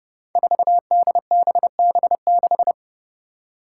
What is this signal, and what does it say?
Morse code, heard as a keyed tone, about 32 wpm, 700 Hz. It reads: 4DBB6
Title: Five Challenging Letter-Number Combinations 30wpm